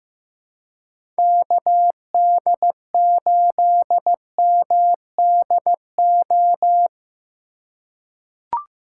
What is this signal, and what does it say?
Morse code, heard as a keyed tone, about 15 wpm, 700 Hz. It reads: KD8MDO
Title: USA Call Signs - 3x Character Spacing - 15wpm